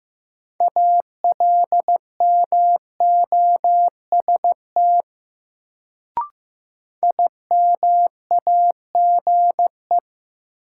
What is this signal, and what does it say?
Morse code, heard as a keyed tone, about 15 wpm, 700 Hz. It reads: ALMOST IMAGE
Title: Sentences from Top 700 Words - New Word Drill - 15wpm